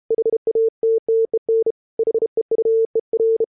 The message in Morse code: HATTEN HEUER